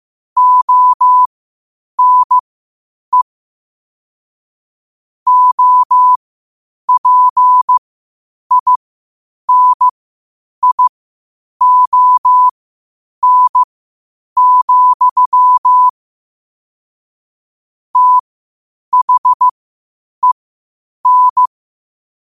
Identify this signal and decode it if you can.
Morse code, heard as a keyed tone, about 15 wpm, spaced as for 8 wpm, 1000 Hz. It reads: ONE OPINION, THEN
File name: qotdfiles / 8wpm